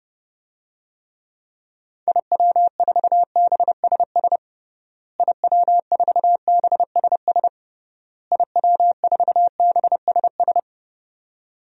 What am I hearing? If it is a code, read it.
Morse code, heard as a keyed tone, about 30 wpm, 700 Hz. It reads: IW4BSS IW4BSS IW4BSS
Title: DXCC-100 Call Signs - 6 Characters - Repeated 3x - 30wpm